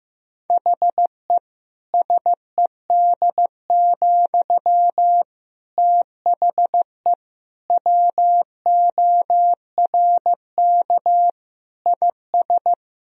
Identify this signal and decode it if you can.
Morse code, heard as a keyed tone, about 15 wpm, 700 Hz. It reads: HE SED, THE WORK IS
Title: Sentences from Top 400 Words and Abbreviations 15wpm